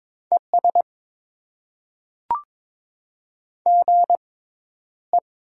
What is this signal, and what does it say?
Morse code, heard as a keyed tone, about 22 wpm, spaced as for 11 wpm, 700 Hz. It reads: ES GE
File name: Top 20 QSO Elements - 6x Character Spacing - 22wpm